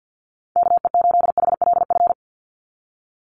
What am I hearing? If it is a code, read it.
Morse code, heard as a keyed tone, about 50 wpm, 700 Hz. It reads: KE8HLF
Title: USA Call Signs 50wpm